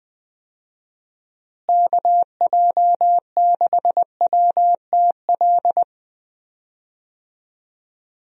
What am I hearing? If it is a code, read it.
Morse code, heard as a keyed tone, about 20 wpm, 700 Hz. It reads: KJ6WTL
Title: USA Call Signs - 6 Characters - Repeated 3x - 20wpm